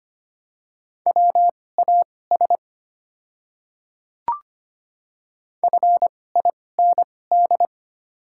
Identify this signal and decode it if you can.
Morse code, heard as a keyed tone, about 25 wpm, spaced as for 18 wpm, 700 Hz. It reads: WAS FIND
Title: 2x Character Spacing - Top 100 Words - 25wpm